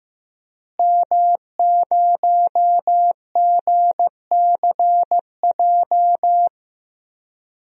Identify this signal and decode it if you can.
Morse code, heard as a keyed tone, about 15 wpm, 700 Hz. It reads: M0GCJ
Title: DXCC-100 Call Signs - 5 Characters - Repeated 2x - 15wpm